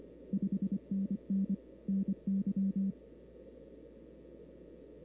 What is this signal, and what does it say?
Morse code, heard as a keyed tone, about 25 wpm, 200 Hz. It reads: 5NN NY
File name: CW / 2005-k5tr arrl cw dx -03:00